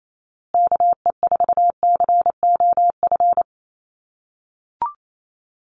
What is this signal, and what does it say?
Morse code, heard as a keyed tone, about 28 wpm, 700 Hz. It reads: KE4COF
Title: USA Call Signs Repeated 3x 28wpm